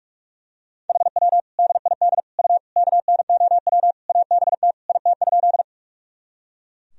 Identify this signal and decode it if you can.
Morse code, heard as a keyed tone, about 41 wpm, 700 Hz. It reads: HW DID U KNOW ABT IT?
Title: Sentences from Top 100 Words and Abbreviations 45wpm